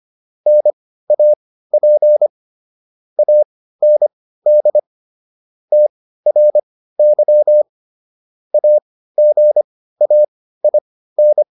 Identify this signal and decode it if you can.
Morse code, heard as a keyed tone, about 25 wpm, spaced as for 15 wpm, 600 Hz. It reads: NAP AND TRY AGAIN